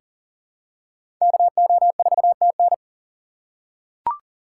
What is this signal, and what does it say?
Morse code, heard as a keyed tone, about 40 wpm, 700 Hz. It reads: KO4TN
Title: USA Call Signs - 6x Character Spacing - 40wpm